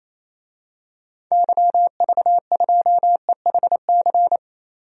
Morse code, heard as a keyed tone, about 28 wpm, 700 Hz. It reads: YV2EHC
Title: DXCC-100 Call Signs - 6 Characters - 28wpm